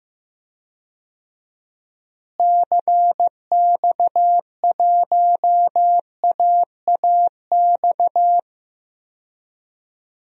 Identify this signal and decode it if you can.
Morse code, heard as a keyed tone, about 15 wpm, 700 Hz. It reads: CX1AAX
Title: DXCC-100 Call Signs - 6 Characters - Repeated 3x - 15wpm